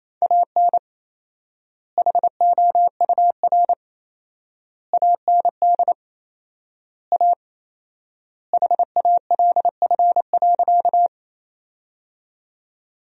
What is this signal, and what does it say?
Morse code, heard as a keyed tone, about 28 wpm, 700 Hz. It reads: AN HOUR AND A HALF.
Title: 4x Word-Spacing - Sentences from Top 400 Words - 28wpm